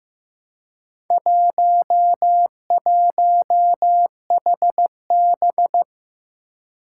Morse code, heard as a keyed tone, about 15 wpm, 700 Hz. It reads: 11HB
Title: Four Challenging Letter-Number Combinations 15wpm